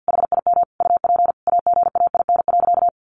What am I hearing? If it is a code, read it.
Morse code, heard as a keyed tone, about 50 wpm, 700 Hz. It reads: HIM UP AGAIN.
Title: Sentences from Top 1400 Words 50wpm